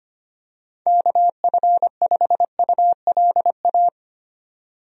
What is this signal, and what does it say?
Morse code, heard as a keyed tone, about 25 wpm, 700 Hz. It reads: KF5ULA
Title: USA Call Signs - 6 Characters - 25wpm